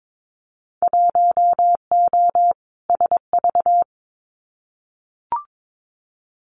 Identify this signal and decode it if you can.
Morse code, heard as a keyed tone, about 22 wpm, 700 Hz. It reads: 1O SV
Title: Winter Field Day Exchanges 22wpm